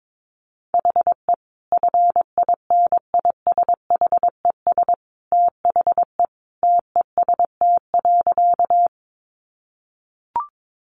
Morse code, heard as a keyed tone, about 22 wpm, 700 Hz. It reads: HE FINISHES THE TEST.
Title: Additional Easy Phrases 22wpm